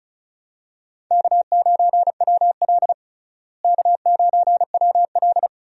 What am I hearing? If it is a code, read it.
Morse code, heard as a keyed tone, about 36 wpm, 700 Hz. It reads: K9WL K9WL
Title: USA Call Signs - 4 Characters - Repeated 2x - 35wpm